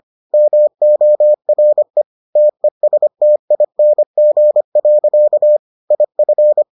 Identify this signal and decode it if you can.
Morse code, heard as a keyed tone, about 25 wpm, 600 Hz. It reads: MORE TESTING. IF